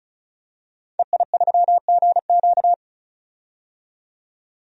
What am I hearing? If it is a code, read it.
Morse code, heard as a keyed tone, about 36 wpm, 700 Hz. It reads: EI3GQ
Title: DXCC-100 Call Signs - 5 Characters - 35wpm